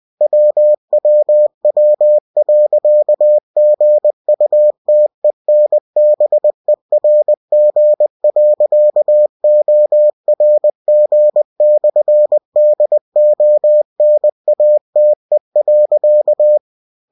Morse code, heard as a keyed tone, about 20 wpm, 600 Hz. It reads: WWW.GUTENBERG.ORG/DONATE.
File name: cw-books / winnie-the-pooh / winnie-20wpm-page-0122